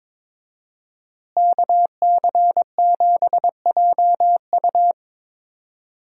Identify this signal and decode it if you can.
Morse code, heard as a keyed tone, about 22 wpm, 700 Hz. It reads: KC7JU
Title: USA Call Signs - 5 Characters - Repeated 2x - 22wpm